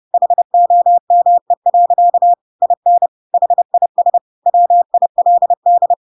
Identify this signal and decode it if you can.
Morse code, heard as a keyed tone, about 30 wpm, 700 Hz. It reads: HOME. IN HIS WILD